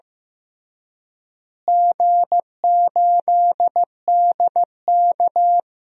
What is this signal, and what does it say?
Morse code, heard as a keyed tone, about 15 wpm, 700 Hz. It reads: G8DK
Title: DXCC-100 Call Signs - 4 Characters - Repeated 3x - 15wpm